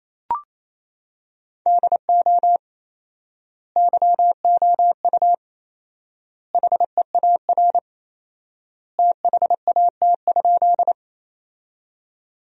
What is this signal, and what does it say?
Morse code, heard as a keyed tone, about 27 wpm, 700 Hz. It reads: DO YOU HEAR THAT?
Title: 4x Word-Spacing - Sentences from Top 300 Words - 28wpm